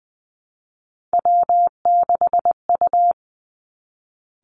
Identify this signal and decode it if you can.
Morse code, heard as a keyed tone, about 20 wpm, 700 Hz. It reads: W6U
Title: Three Challenging Letter-Number Combinations 20wpm